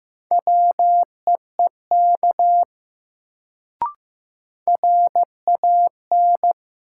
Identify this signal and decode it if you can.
Morse code, heard as a keyed tone, about 15 wpm, 700 Hz. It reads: WEEK RAN
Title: Sentences from Top 500 Words - New Word Drill - 15wpm